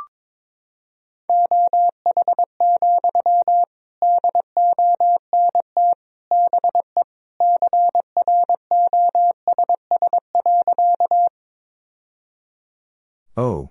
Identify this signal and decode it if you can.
Morse code, heard as a keyed tone, about 21 wpm, 700 Hz. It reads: OH, DONT BE CROSS.
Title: Sentences from Top 1000 Words - Review Part 1 - 22wpm